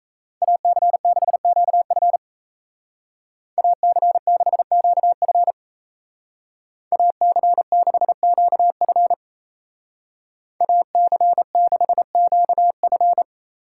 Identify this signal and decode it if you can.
Morse code, heard as a keyed tone, about 33 wpm, 700 Hz. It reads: AC6QF AC6QF AC6QF AC6QF
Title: Speed Racing - USA 2x2 Call Signs 28wpm